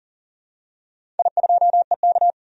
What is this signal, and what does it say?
Morse code, heard as a keyed tone, about 40 wpm, 700 Hz. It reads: I2EK
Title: DXCC-100 Call Signs - 4 Characters - 40wpm